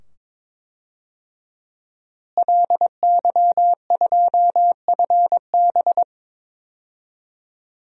Morse code, heard as a keyed tone, about 22 wpm, 700 Hz. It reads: LY2FB
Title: DXCC-100 Call Signs - 5 Characters - Repeated 2x - 22wpm